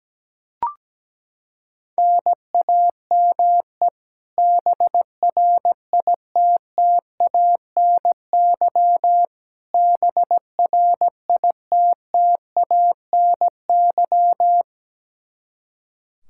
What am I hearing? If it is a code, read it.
Morse code, heard as a keyed tone, about 17 wpm, 700 Hz. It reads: NAME BRITTANY BRITTANY
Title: Practice QSO - Exchange 1 - Beginning - 17wpm